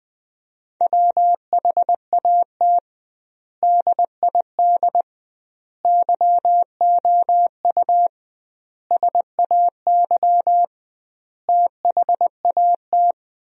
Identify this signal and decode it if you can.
Morse code, heard as a keyed tone, about 21 wpm, 700 Hz. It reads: WHAT DID YOU SAY THAT
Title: 2x Word-Spacing - Sentences from Top 200 Words - 20wpm